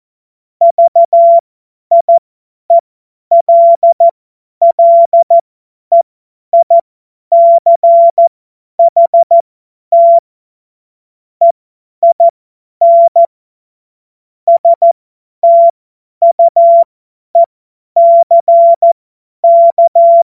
Morse code, heard as a keyed tone, about 14 wpm, spaced as for 10 wpm, 680 Hz. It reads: VIELLEICHT EIN STUECK
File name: Morsekurs / Witze / Witz 09. Februar 2021, 50 F70